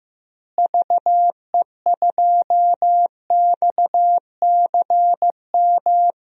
Morse code, heard as a keyed tone, about 15 wpm, 700 Hz. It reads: VE2XCM
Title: DXCC-100 Call Signs - 6 Characters - Repeated 2x - 15wpm